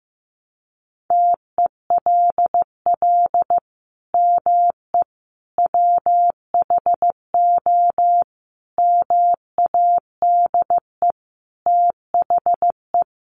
Morse code, heard as a keyed tone, about 15 wpm, 700 Hz. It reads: TELL ME WHO MADE THE
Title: Sentences from Top 200 Words 15wpm